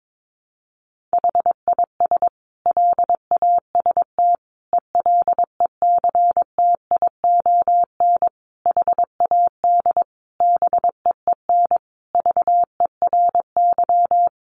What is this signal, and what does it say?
Morse code, heard as a keyed tone, about 22 wpm, 700 Hz. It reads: HIS LAST ELECTION HAD BEEN VERY
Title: Sentences from Top 1500 Words 22wpm